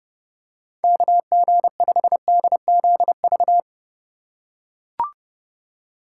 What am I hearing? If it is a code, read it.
Morse code, heard as a keyed tone, about 30 wpm, 700 Hz. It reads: KG5DZV
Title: USA Call Signs - 6 Characters - 4x Character Spacing - 30wpm